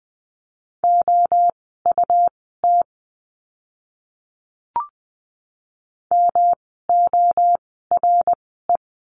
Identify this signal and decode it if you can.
Morse code, heard as a keyed tone, about 20 wpm, spaced as for 15 wpm, 700 Hz. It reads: OUT MORE
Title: 160 Lesson 26f - Letter g - Words 20wpm